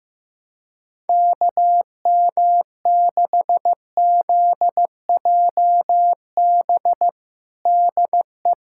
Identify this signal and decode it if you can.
Morse code, heard as a keyed tone, about 16 wpm, 700 Hz. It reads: KM6ZJB DE
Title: USA Callsign DE Callsign 15wpm